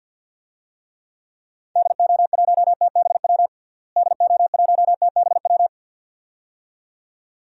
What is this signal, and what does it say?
Morse code, heard as a keyed tone, about 47 wpm, 700 Hz. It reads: DO1TBW DO1TBW
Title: DXCC-100 Call Signs - 6 Characters - Repeated 2x - 50wpm